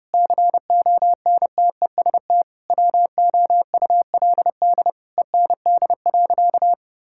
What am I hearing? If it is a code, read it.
Morse code, heard as a keyed tone, about 29 wpm, 700 Hz. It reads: CONTEST WOULD END.
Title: Sentences from Top 1600 Words 30wpm